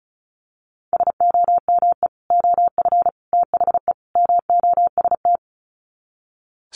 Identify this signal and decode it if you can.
Morse code, heard as a keyed tone, about 34 wpm, 700 Hz. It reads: SOME OF THE MOST
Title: Sets of 4 Words 35wpm